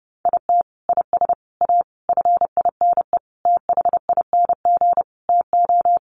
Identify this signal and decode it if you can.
Morse code, heard as a keyed tone, about 30 wpm, 700 Hz. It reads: IT IS A FINE THING TO
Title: Sentences from Top 1000 Words 30wpm